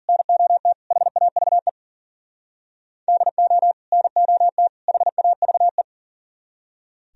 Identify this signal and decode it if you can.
Morse code, heard as a keyed tone, about 50 wpm, 700 Hz. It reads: NOT HAVE DO NOT HAVE